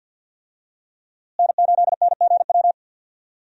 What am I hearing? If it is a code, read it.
Morse code, heard as a keyed tone, about 47 wpm, 700 Hz. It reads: N8NGW
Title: USA Call Signs - 5 Characters - Repeated 2x - 50wpm